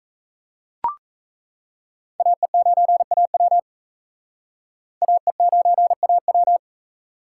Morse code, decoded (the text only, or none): AE9AW AE9AW